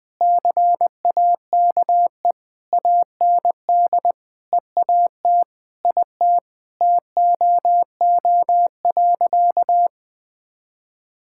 Morse code, heard as a keyed tone, about 20 wpm, 700 Hz. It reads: CAKE AND EAT IT TOO.
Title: Sentences with 8-Word Phrases - 20wpm